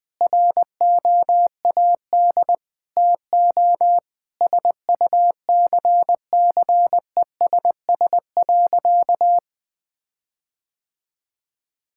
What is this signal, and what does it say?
Morse code, heard as a keyed tone, about 20 wpm, 700 Hz. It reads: ROAD TO SUCCESS.
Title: Difficult Sentences with Hints - 20wpm